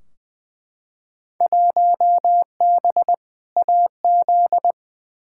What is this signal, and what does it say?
Morse code, heard as a keyed tone, about 20 wpm, 700 Hz. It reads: 1B AZ
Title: Field Day Exchanges Encore - 20wpm